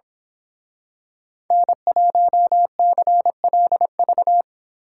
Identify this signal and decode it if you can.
Morse code, heard as a keyed tone, about 26 wpm, 700 Hz. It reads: N1CLV
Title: Speed Racing - USA 1x3 Call Signs 22wpm